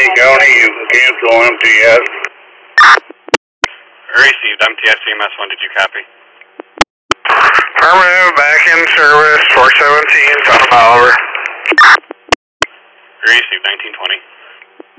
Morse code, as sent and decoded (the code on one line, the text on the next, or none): none